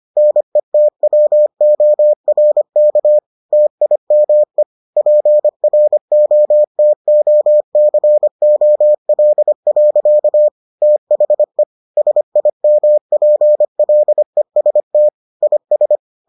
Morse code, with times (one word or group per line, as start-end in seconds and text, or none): 0.2-3.2 NETWORK
3.5-4.6 TIME
5.0-10.5 PROTOCOL.
10.8-11.6 THE
12.0-15.1 SIMPLEST
15.4-16.0 IS